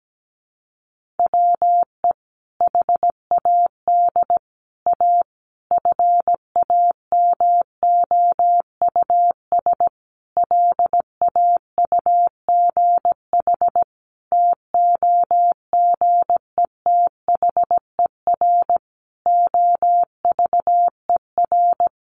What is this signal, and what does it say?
Morse code, heard as a keyed tone, about 17 wpm, 700 Hz. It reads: WE HAD A FAMOUS LAUGH TOGETHER OVER
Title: Sentences from Top 1500 Words 17wpm